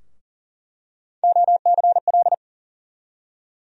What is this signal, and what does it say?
Morse code, heard as a keyed tone, about 40 wpm, 700 Hz. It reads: OCL
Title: USA Call Sign Postfixes - 3 Letters 40wpm